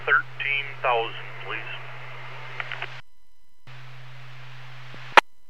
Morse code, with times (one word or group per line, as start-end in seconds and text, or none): none